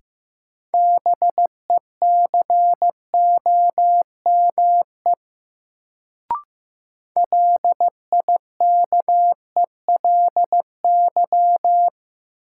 Morse code, heard as a keyed tone, about 15 wpm, 700 Hz. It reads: BECOME LIKELY
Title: Sentences from Top 600 Words - New Word Drill - 15wpm